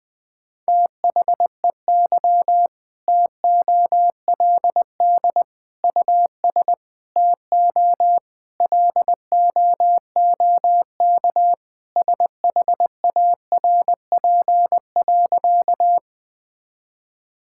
Morse code, heard as a keyed tone, about 20 wpm, 700 Hz. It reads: THEY TOLD US TO LOOK SHARP.
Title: Sentences from Top 1600 Words 20wpm